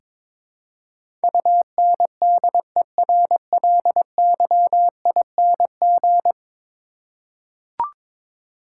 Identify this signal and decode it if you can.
Morse code, heard as a keyed tone, about 22 wpm, 700 Hz. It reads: UNDERLYING
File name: practice / 22wpm / 10 Letter Words - First 3 Letters Given - 22wpm